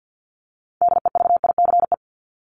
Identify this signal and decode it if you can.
Morse code, heard as a keyed tone, about 50 wpm, 700 Hz. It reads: DEVICE
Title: Top 1251-1500 Words 50wpm